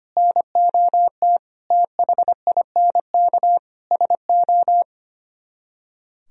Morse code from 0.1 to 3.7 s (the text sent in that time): NOT THINK